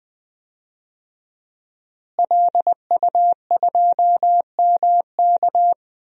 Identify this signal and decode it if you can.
Morse code, heard as a keyed tone, about 20 wpm, 700 Hz. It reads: LU2MK
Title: DXCC-100 Call Signs - 5 Characters - Repeated 3x - 20wpm